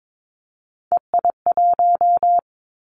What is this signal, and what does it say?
Morse code, heard as a keyed tone, about 22 wpm, 700 Hz. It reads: EI1
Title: Callsign Prefixes for Top 50 DXCC - 22wpm